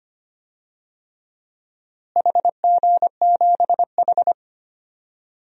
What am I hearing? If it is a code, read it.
Morse code, heard as a keyed tone, about 25 wpm, 700 Hz. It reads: HG7H
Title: DXCC-100 Call Signs - 4 Characters - Repeated 3x - 25wpm